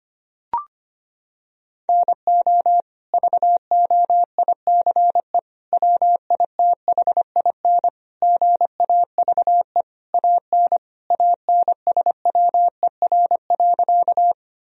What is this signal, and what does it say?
Morse code, heard as a keyed tone, about 25 wpm, 700 Hz. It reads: NO VOICE WITHIN GAVE AN ANSWER.
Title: Sentences from Top 1000 Words - Review Part 4 - 25wpm